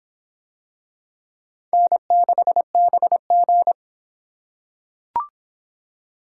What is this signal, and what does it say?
Morse code, heard as a keyed tone, about 26 wpm, 700 Hz. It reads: N6BG